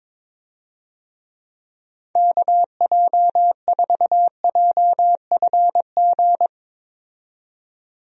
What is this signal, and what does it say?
Morse code, heard as a keyed tone, about 21 wpm, 700 Hz. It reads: KJ4JFG
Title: USA Call Signs - 6 Characters - 22wpm